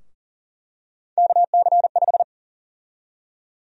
Morse code, heard as a keyed tone, about 40 wpm, 700 Hz. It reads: KC5
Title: USA Call Sign Prefixes - 40wpm